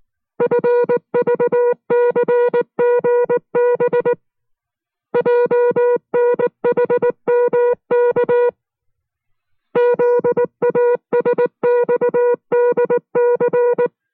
Morse code, heard as a keyed tone, about 19 wpm, 480 Hz. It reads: FVCGB JNHMK ZASXDC